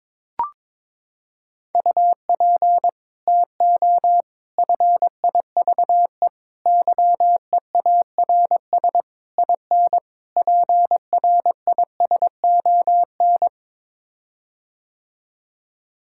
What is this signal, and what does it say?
Morse code, heard as a keyed tone, about 22 wpm, 700 Hz. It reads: UP TO FIVE YEARS IN PRISON
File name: practice / 22wpm / Sets of 6 Words - 22wpm